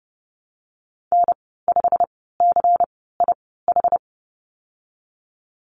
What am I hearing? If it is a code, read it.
Morse code, heard as a keyed tone, about 30 wpm, spaced as for 17 wpm, 700 Hz. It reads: N5CIH
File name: Lesson 19g - Letter w - Call Signs 30wpm